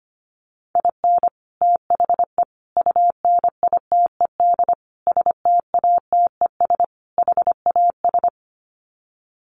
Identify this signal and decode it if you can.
Morse code, heard as a keyed tone, about 25 wpm, 700 Hz. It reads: IN THE UNITED STATES HAS
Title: Sets of 5 Words Encore2 25wpm